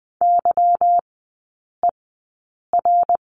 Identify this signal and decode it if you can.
Morse code, heard as a keyed tone, about 20 wpm, spaced as for 8 wpm, 690 Hz. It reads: YER
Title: 191 Lesson 31e - Letter v - Three Characters 20wpm